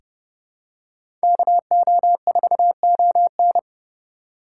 Morse code, heard as a keyed tone, about 30 wpm, 700 Hz. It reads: KO4ON
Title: USA Call Signs - 5 Characters - 2x Character Spacing - 30wpm